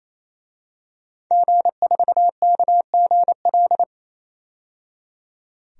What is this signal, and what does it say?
Morse code, heard as a keyed tone, about 28 wpm, 700 Hz. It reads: G4KGL
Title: DXCC-100 Call Signs - 5 Characters - 28wpm